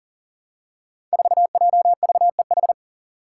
Morse code, heard as a keyed tone, about 40 wpm, 700 Hz. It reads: 4JVEH
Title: Five Challenging Letter-Number Combinations 40wpm